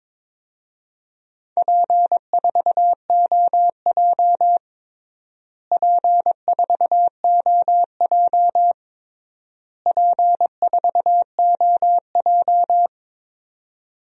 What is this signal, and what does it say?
Morse code, heard as a keyed tone, about 22 wpm, 700 Hz. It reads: P4OJ P4OJ P4OJ
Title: DXCC-100 Call Signs - 4 Characters - Repeated 3x - 22wpm